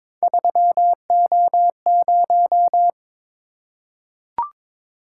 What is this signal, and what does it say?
Morse code, heard as a keyed tone, about 22 wpm, 700 Hz. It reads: 3O0
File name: practice / 22wpm / Three Letter-Number Combinations - 22wpm